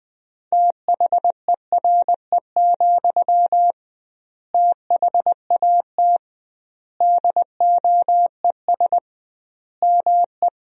Morse code, heard as a keyed tone, about 20 wpm, 700 Hz. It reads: THERE, THAT DOES ME
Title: 2x Word-Spacing - Sentences from Top 200 Words - 20wpm